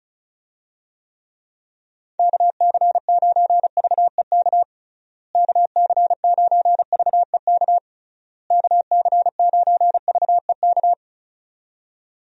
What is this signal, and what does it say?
Morse code, heard as a keyed tone, about 34 wpm, 700 Hz. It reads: KC9VEK KC9VEK KC9VEK
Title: USA Call Signs - 6 Characters - Repeated 3x - 35wpm